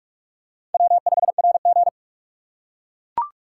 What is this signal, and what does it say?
Morse code, heard as a keyed tone, about 43 wpm, 700 Hz. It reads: W5RG
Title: USA Call Signs - 4 Characters - 2x Character Spacing - 45wpm